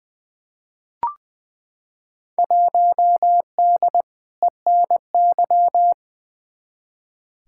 Morse code, heard as a keyed tone, about 20 wpm, 700 Hz. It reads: 1D ENY
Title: Field Day Exchanges - 20wpm